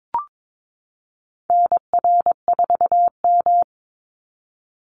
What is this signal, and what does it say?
Morse code, heard as a keyed tone, about 22 wpm, 700 Hz. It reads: NR4M
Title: Call Signs Repeated 2x - 22wpm